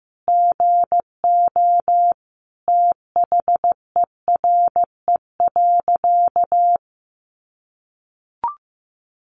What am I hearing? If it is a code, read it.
Morse code, heard as a keyed tone, about 15 wpm, 700 Hz. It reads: GO THERE.
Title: Sentences from Top 800 Words 15wpm